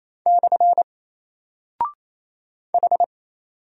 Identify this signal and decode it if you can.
Morse code, heard as a keyed tone, about 28 wpm, 700 Hz. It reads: / H
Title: Single Character (Letter:Number:Punctuation:Prosign) 28wpm